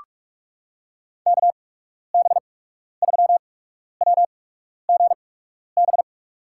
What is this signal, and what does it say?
Morse code, heard as a keyed tone, about 46 wpm, spaced as for 12 wpm, 700 Hz. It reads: KB3WGB
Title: USA Call Signs - 6 Characters - 8x Character Spacing - 45wpm